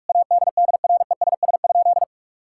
Morse code, heard as a keyed tone, about 42 wpm, 700 Hz. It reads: ADDRESS?
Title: Sentences from Top 1000 Words - Review Part 6 - 45wpm